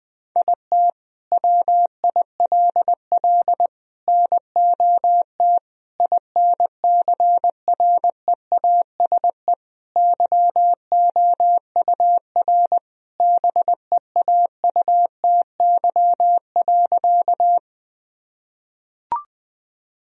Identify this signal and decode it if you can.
Morse code, heard as a keyed tone, about 20 wpm, 700 Hz. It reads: IT WILL NOT INCREASE YOUR BEAUTY.
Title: Sentences from Top 1000 Words - Review Part 4 - 20wpm